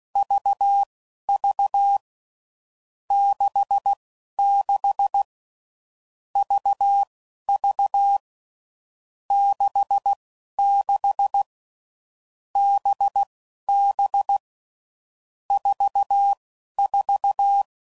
Morse code, heard as a keyed tone, about 16 wpm, spaced as for 11 wpm, 800 Hz. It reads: VV 66 VV 66 BB 44
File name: Groups / Basic Letter Groups / VB46